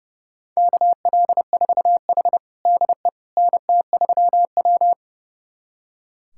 Morse code, heard as a keyed tone, about 30 wpm, 700 Hz. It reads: KL4H DE NT3W
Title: USA Callsign DE Callsign - 4 Characters - 30wpm